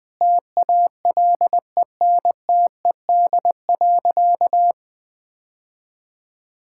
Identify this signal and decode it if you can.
Morse code, heard as a keyed tone, about 20 wpm, 700 Hz. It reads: TALENTED.
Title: Sentences with 5-Word Phrases - 20wpm